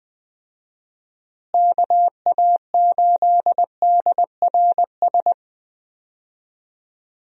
Morse code, heard as a keyed tone, about 20 wpm, 700 Hz. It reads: KA8DRS
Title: USA Call Signs - 6 Characters - Repeated 2x - 20wpm